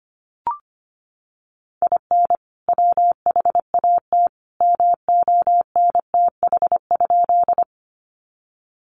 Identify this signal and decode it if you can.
Morse code, heard as a keyed tone, about 25 wpm, 700 Hz. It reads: IN WHAT MONTH?
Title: Sentences from Top 800 Words 25wpm